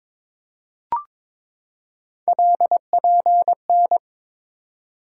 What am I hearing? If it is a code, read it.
Morse code, heard as a keyed tone, about 22 wpm, 700 Hz. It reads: LPN
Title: K1USN SST QSO Practice 22wpm